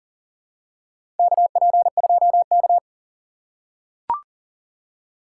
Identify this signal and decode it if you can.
Morse code, heard as a keyed tone, about 38 wpm, 700 Hz. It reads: KP2K
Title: USA Call Signs - 4 Characters - 3x Character Spacing - 40wpm